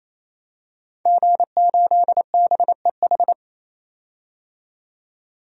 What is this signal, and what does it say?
Morse code, heard as a keyed tone, about 28 wpm, 700 Hz. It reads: G8BEH
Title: DXCC-100 Call Signs - 5 Characters - 28wpm